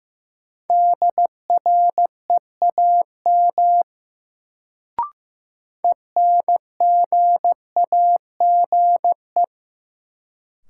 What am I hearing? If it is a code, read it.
Morse code, heard as a keyed tone, about 15 wpm, 700 Hz. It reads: DREAM ENGAGE
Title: Sentences from Top 1500 Words - New Word Drill Alt - 15wpm